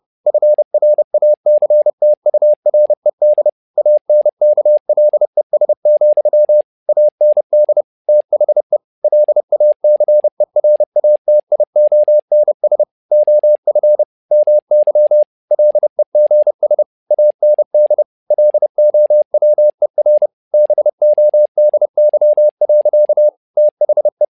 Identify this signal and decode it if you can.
Morse code, heard as a keyed tone, about 30 wpm, 600 Hz. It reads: FRACTURED ANKLES, AND THE LACERATIONS OF MY LEGS AND LOWER BODY. THE
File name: learning center / tower trouble / tower trouble 30wpm